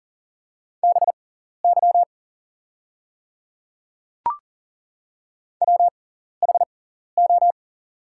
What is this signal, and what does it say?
Morse code, heard as a keyed tone, about 39 wpm, spaced as for 14 wpm, 700 Hz. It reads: BY WHO
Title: 6x Character Spacing - Top 100 Words - 40wpm